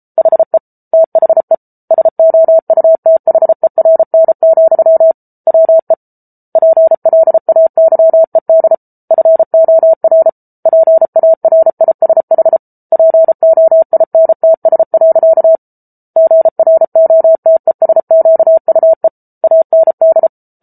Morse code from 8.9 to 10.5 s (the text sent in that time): FOR